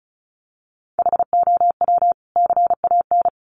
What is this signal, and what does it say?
Morse code, heard as a keyed tone, about 34 wpm, 700 Hz. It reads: HOW CAN
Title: Sentences from Top 1300 Words 35wpm